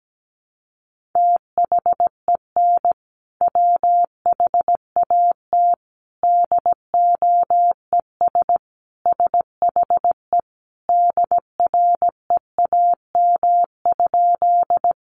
Morse code, heard as a keyed tone, about 17 wpm, 700 Hz. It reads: THEN WHAT DOES SHE DREAM?
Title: Sentences from Top 1500 Words 17wpm